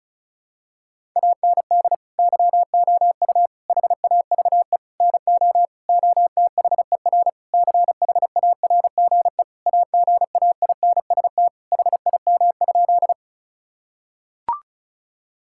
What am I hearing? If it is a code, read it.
Morse code, heard as a keyed tone, about 36 wpm, 700 Hz. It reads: AND YOU HAVE NO OTHER CHARGE AGAINST HIM?
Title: Sentences from Top 1000 Words - Review Part 6 - 35wpm